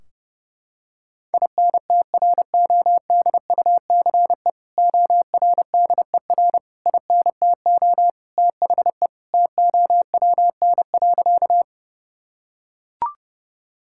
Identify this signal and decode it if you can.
Morse code, heard as a keyed tone, about 30 wpm, 700 Hz. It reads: INTRODUCE ORDER INTO THE TOWN.
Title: Sentences from Top 1600 Words 30wpm